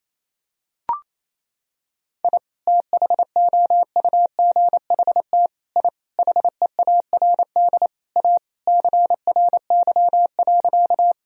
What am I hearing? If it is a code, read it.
Morse code, heard as a keyed tone, about 28 wpm, 700 Hz. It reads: I THOUGHT I HEARD A CRY.
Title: Sentences from Top 1000 Words - Review Part 2 - 28wpm